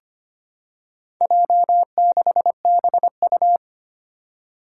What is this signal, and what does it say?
Morse code, heard as a keyed tone, about 25 wpm, 700 Hz. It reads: J6BU
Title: Four Challenging Letter-Number Combinations 25wpm